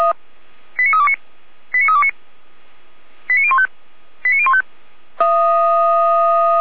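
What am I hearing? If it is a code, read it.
Five-tone selective call (ZVEI), heard as quick runs of five tones, tones of about 70 ms. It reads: 89329 89329 89015 89015
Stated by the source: ZVEI five-tone call received with an SDR.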